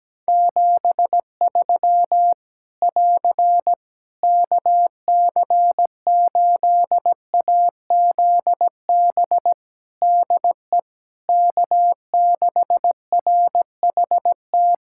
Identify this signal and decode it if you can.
Morse code, heard as a keyed tone, about 17 wpm, 700 Hz. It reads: 73 <AR> KC8AZB DE K6RHT
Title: Practice QSO - Exchange 4 - Ending - 17wpm